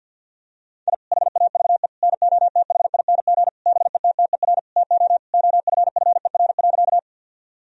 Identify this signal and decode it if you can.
Morse code, heard as a keyed tone, about 50 wpm, 700 Hz. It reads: I HAVE NOTHING BETTER TO OFFER.
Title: Sentences from Top 1000 Words - Review Part 4 - 50wpm